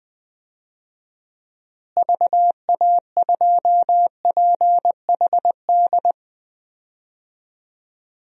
Morse code, heard as a keyed tone, about 20 wpm, 700 Hz. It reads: VA2PHD
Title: DXCC-100 Call Signs - 6 Characters - Repeated 2x - 20wpm